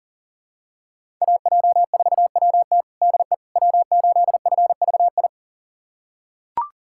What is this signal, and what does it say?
Morse code, heard as a keyed tone, about 40 wpm, 700 Hz. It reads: AJ4WT DE W8FVI
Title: USA Callsign DE Callsign - 5 Characters - 40wpm